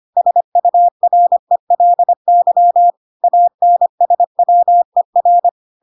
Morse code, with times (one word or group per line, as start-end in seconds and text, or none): 0.2-2.9 SURELY
3.2-5.5 ANSWER